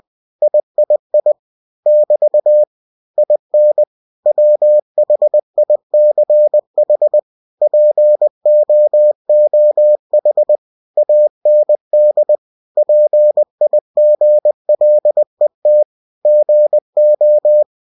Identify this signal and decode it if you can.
Morse code, heard as a keyed tone, about 20 wpm, 600 Hz. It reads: III <BT> IN WHICH POOH AND PIGLET GO
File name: cw-books / winnie-the-pooh / winnie-20wpm-page-0020